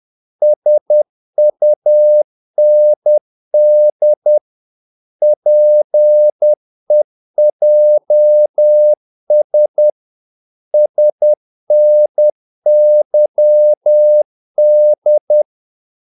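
Morse code, heard as a keyed tone, about 10 wpm, 600 Hz. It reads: SUND PEJS SNYD
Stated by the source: Morse practice of short Danish words at 10 wpm, from CW med Gnister.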